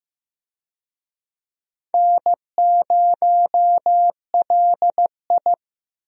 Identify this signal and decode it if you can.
Morse code, heard as a keyed tone, about 15 wpm, 700 Hz. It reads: N0LI
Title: USA Call Signs - 4 Characters - Repeated 3x - 15wpm